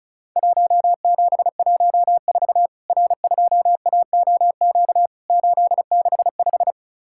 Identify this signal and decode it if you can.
Morse code, heard as a keyed tone, about 36 wpm, 700 Hz. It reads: 1714 R2AOQ 865
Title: Worked All Europe DX Contest - QTC Exchange - 35wpm